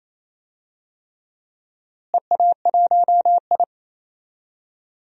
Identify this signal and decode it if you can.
Morse code, heard as a keyed tone, about 28 wpm, 700 Hz. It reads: EA1I
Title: DXCC-100 Call Signs - 4 Characters - Repeated 3x - 28wpm